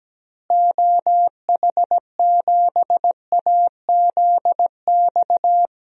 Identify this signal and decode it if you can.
Morse code, heard as a keyed tone, about 17 wpm, 700 Hz. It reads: OH7AZX
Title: DXCC-100 Call Signs - 6 Characters - Repeated 2x - 17wpm